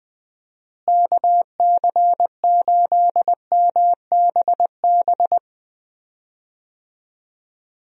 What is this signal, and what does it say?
Morse code, heard as a keyed tone, about 20 wpm, 700 Hz. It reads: KC8MBB
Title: USA Call Signs - 6 Characters - Repeated 2x - 20wpm